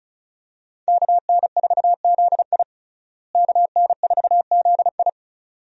Morse code, heard as a keyed tone, about 36 wpm, 700 Hz. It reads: KN4ZI KN4ZI
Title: USA Call Signs - 5 Characters - Repeated 2x - 35wpm